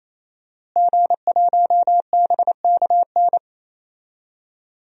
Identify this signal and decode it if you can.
Morse code, heard as a keyed tone, about 28 wpm, 700 Hz. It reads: G1BKN
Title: DXCC-100 Call Signs - 5 Characters - Repeated 3x - 28wpm